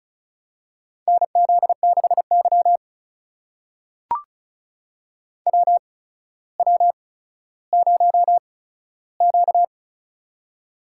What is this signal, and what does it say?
Morse code, heard as a keyed tone, about 34 wpm, spaced as for 15 wpm, 700 Hz. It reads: NZ6Y WW0Q
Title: USA Call Signs - 4 Characters - 8x Character Spacing - 35wpm